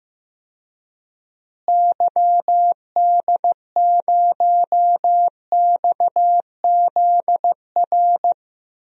Morse code, heard as a keyed tone, about 16 wpm, 700 Hz. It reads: YD0XZR
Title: DXCC-100 Call Signs - 6 Characters - Repeated 2x - 15wpm